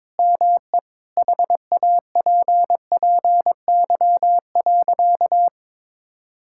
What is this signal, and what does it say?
Morse code, heard as a keyed tone, about 22 wpm, 700 Hz. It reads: ME HAPPY.
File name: Sentences with 7-Word Phrases - 22wpm